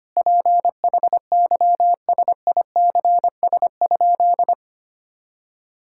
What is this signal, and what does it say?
Morse code, heard as a keyed tone, about 25 wpm, 700 Hz. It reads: PHYSICS?
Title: Sentences from Top 1000 Words - Review Part 5 - 25wpm